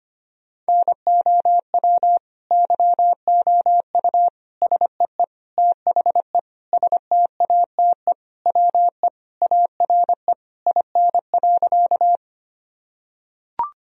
Morse code, heard as a keyed tone, about 25 wpm, 700 Hz. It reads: NOW YOU SEE THE STATE WE ARE IN.
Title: Sentences from Top 1000 Words - Review Part 2 - 25wpm